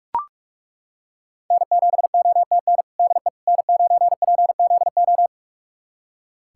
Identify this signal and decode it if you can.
Morse code, heard as a keyed tone, about 41 wpm, 700 Hz. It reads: N7OTN DE N9PZO
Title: USA Callsign DE Callsign - 5 Characters - 45wpm